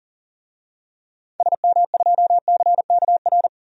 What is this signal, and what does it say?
Morse code, heard as a keyed tone, about 40 wpm, 700 Hz. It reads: SM2CKR
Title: DXCC-100 Call Signs - 6 Characters - 40wpm